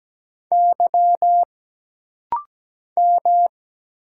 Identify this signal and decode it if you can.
Morse code, heard as a keyed tone, about 17 wpm, 700 Hz. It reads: Y M
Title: Single Letters - Rapid-Fire 17wpm